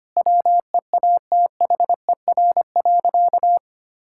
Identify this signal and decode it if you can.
Morse code, heard as a keyed tone, about 25 wpm, 700 Hz. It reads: WEATHER.
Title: Sentences from Top 1000 Words - Review Part 5 - 25wpm